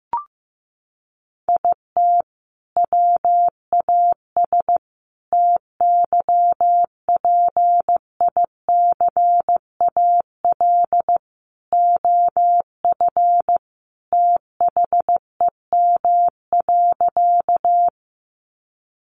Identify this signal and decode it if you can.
Morse code, heard as a keyed tone, about 15 wpm, 700 Hz. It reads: IT WAS TYPICAL OF THEM.
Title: Sentences from Top 1100 Words 15wpm